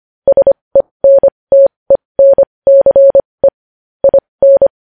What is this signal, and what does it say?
Morse code, heard as a keyed tone, about 25 wpm, spaced as for 20 wpm, 550 Hz. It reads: SENTENCE IN
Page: News Headlines in Morse Code at 20 WPM